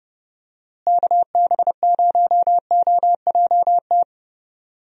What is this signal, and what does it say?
Morse code, heard as a keyed tone, about 32 wpm, 700 Hz. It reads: KB0OJT
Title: USA Call Signs - 6 Characters - Repeated 2x - 30wpm